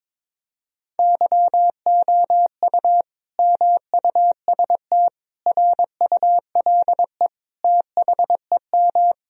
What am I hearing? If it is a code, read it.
Morse code, heard as a keyed tone, about 22 wpm, 700 Hz. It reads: YOU MUST RULE THEM
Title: Sentences from Top 1000 Words - Review Part 2 - 22wpm